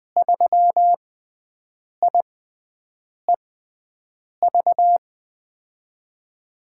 Morse code, heard as a keyed tone, about 20 wpm, spaced as for 7 wpm, 700 Hz. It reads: 3IEV
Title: USA Call Signs - 5 Characters - 6x Character Spacing - 20wpm